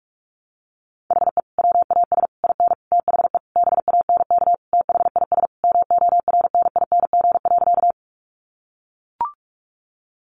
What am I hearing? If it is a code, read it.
Morse code, heard as a keyed tone, about 44 wpm, 700 Hz. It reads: HE WAS IN THE BANK THIS MORNING.
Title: Sentences from Top 900 Words 45wpm